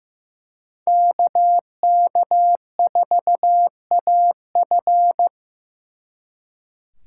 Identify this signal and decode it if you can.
Morse code, heard as a keyed tone, about 18 wpm, 700 Hz. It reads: KK4AF
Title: Speed Racing - USA 2x2 Call Signs 15wpm